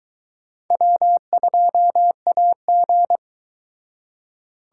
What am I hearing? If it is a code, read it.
Morse code, heard as a keyed tone, about 22 wpm, 700 Hz. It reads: W2AG